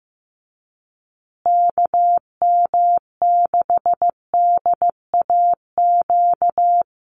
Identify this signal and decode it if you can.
Morse code, heard as a keyed tone, about 15 wpm, 700 Hz. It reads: KM6DAQ
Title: USA Call Signs Repeated 3x 15wpm